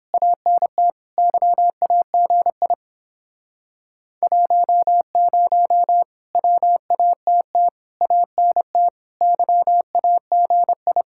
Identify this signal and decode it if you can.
Morse code, heard as a keyed tone, about 30 wpm, 700 Hz. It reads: ANT YAGI 10 WATT ANT YAGI